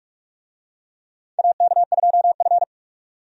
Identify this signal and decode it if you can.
Morse code, heard as a keyed tone, about 46 wpm, 700 Hz. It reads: AK2F
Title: USA Call Signs - 4 Characters - 45wpm